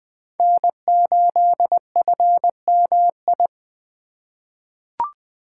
Morse code, heard as a keyed tone, about 21 wpm, 700 Hz. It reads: N8FMI
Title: USA Call Signs - 5 Characters - 2x Character Spacing - 20wpm